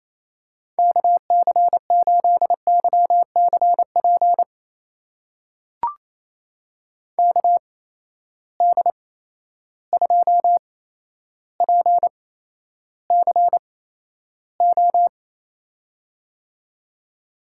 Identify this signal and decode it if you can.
Morse code, heard as a keyed tone, about 28 wpm, spaced as for 12 wpm, 700 Hz. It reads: KC8YCP KD2PCO
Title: USA Call Signs - 6 Characters - 8x Character Spacing - 28wpm